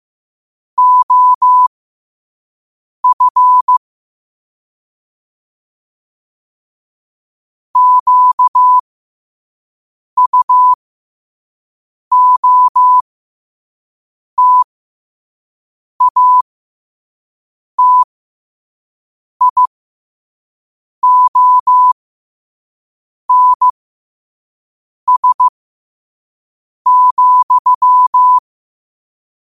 Morse code, heard as a keyed tone, about 15 wpm, spaced as for 5 wpm, 1000 Hz. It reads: OF QUOTATIONS,
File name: qotdfiles / 5wpm